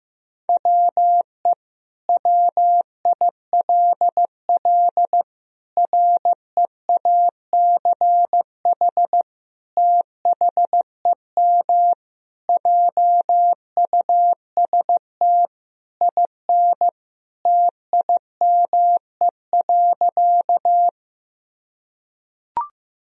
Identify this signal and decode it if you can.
Morse code, heard as a keyed tone, about 16 wpm, 700 Hz. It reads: WE WILL REACH THEM JUST IN TIME.
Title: Sentences from Top 1000 Words - Review Part 2 - 15wpm